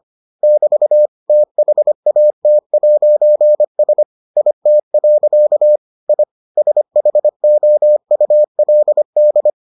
Morse code, heard as a keyed tone, about 25 wpm, 600 Hz. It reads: <BT> THAT'S IT. I SHOULD